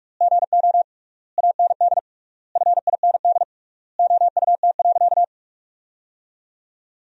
Morse code, heard as a keyed tone, about 46 wpm, 700 Hz. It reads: GO AND FIND OUT.
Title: 3x Word-Spacing - Sentences from Top 200 Words - 45wpm